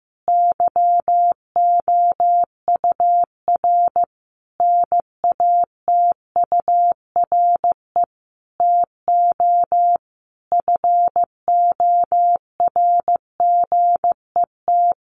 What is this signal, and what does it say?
Morse code, heard as a keyed tone, about 15 wpm, 700 Hz. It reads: YOUR NATURE TO FORGET
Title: Sentences from Top 1200 Words 15wpm